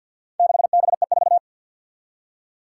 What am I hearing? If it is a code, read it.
Morse code, heard as a keyed tone, about 52 wpm, 700 Hz. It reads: 6BE4
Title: Four Challenging Letter-Number Combinations 50wpm